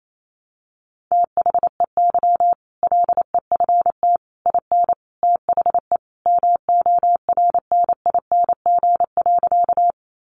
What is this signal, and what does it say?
Morse code, heard as a keyed tone, about 28 wpm, 700 Hz. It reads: THEY LEFT IN THE MORNING.
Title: Sentences from Top 400 Words 28wpm